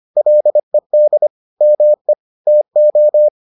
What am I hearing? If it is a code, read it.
Morse code, heard as a keyed tone, about 25 wpm, 600 Hz. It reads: LED ME TO